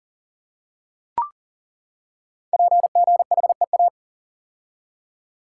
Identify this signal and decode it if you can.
Morse code, heard as a keyed tone, about 42 wpm, 690 Hz. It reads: PGHEA
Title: Code Groups - 5 Letters - 40wpm